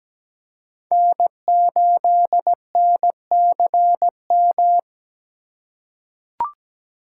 Morse code, heard as a keyed tone, about 17 wpm, 700 Hz. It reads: N8NCM
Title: USA Call Signs - 5 Characters - 4x Character Spacing - 17wpm